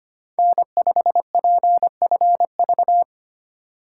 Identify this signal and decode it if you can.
Morse code, heard as a keyed tone, about 25 wpm, 700 Hz. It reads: N5PFV
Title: USA Call Signs - 2x Character Spacing - 25wpm